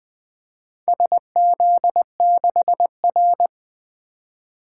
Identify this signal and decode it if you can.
Morse code, heard as a keyed tone, about 20 wpm, 700 Hz. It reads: SZ6R
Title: Four Letter-Number Combinations - 20wpm